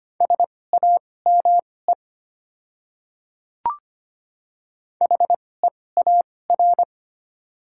Morse code, heard as a keyed tone, about 25 wpm, spaced as for 18 wpm, 700 Hz. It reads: SAME HEAR
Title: 2x Character Spacing - Top 300 Words - 25wpm